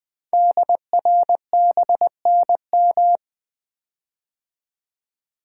Morse code, heard as a keyed tone, about 20 wpm, 700 Hz. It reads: DRBNM
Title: Code Groups - 5 Letters - 20wpm